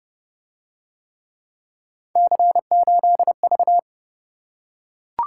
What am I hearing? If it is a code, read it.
Morse code, heard as a keyed tone, about 29 wpm, 700 Hz. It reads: C8V